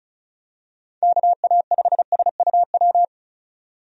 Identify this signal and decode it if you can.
Morse code, heard as a keyed tone, about 36 wpm, 700 Hz. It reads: KA5SUW
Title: USA Call Signs - 6 Characters - 35wpm